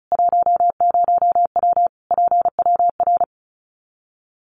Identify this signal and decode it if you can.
Morse code, heard as a keyed tone, about 34 wpm, 700 Hz. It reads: 10W PWR
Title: Common QSO Phrases 35wpm